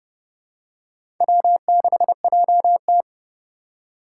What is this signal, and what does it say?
Morse code, heard as a keyed tone, about 28 wpm, 700 Hz. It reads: W6JT